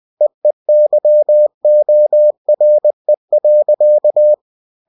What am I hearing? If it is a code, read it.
Morse code, heard as a keyed tone, about 20 wpm, 600 Hz. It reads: EEYORE.
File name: cw-books / winnie-the-pooh / winnie-20wpm-page-0046